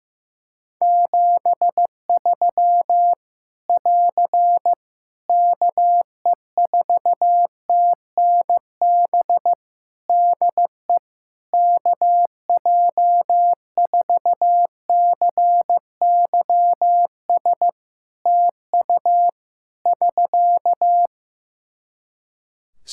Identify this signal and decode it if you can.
Morse code, heard as a keyed tone, about 15 wpm, 700 Hz. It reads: 73 <AR> KE4TNB DE KJ4CYS TU <SK>
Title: Practice QSO - Exchange 4 - Ending - 15wpm